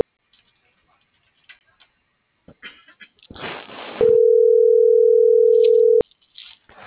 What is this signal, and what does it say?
Ringback tone (North America), 440 + 480 Hz.